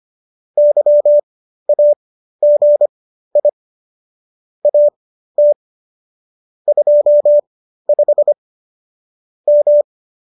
Morse code, heard as a keyed tone, about 25 wpm, spaced as for 13 wpm, 590 Hz. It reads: YAGI AT 25 M